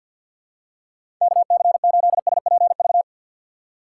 Morse code, heard as a keyed tone, about 52 wpm, 700 Hz. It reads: KC8SPV
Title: USA Call Signs - 6 Characters - 3x Character Spacing - 50wpm